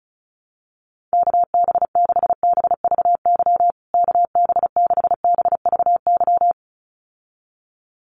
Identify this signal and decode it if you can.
Morse code, heard as a keyed tone, about 34 wpm, 700 Hz. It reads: KB6BVY KB6BVY
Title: USA Call Signs Repeated 2x 35wpm